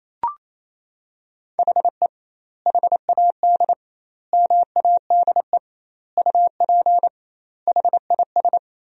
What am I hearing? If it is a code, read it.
Morse code, heard as a keyed tone, about 27 wpm, 700 Hz. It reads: HE HAD MADE UP HIS
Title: 2x Word-Spacing - Sentences from Top 500 Words - 28wpm